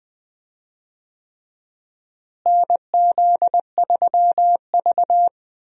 Morse code, heard as a keyed tone, about 20 wpm, 700 Hz. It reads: NZ3V
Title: USA Call Signs - 4 Characters - Repeated 3x - 20wpm